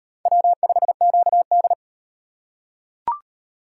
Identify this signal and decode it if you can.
Morse code, heard as a keyed tone, about 33 wpm, 700 Hz. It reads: W5QD